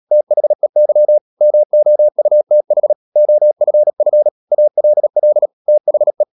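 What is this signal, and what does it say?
Morse code, heard as a keyed tone, about 37 wpm, 600 Hz. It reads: THEY MOUTH OFF ALL THE
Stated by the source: Morse code practice at 35–39 wpm sending English text of Ken Follett's novel Never, from CW med Gnister.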